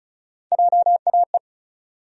Morse code, heard as a keyed tone, about 36 wpm, 700 Hz. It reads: JAE
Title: USA Call Sign Postfixes - 3 Letters 35wpm